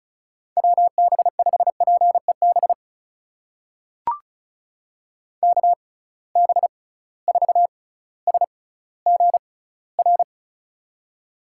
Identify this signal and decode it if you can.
Morse code, heard as a keyed tone, about 36 wpm, spaced as for 18 wpm, 700 Hz. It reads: WB5PEB KB4SGR
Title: USA Call Signs - 6 Characters - 6x Character Spacing - 35wpm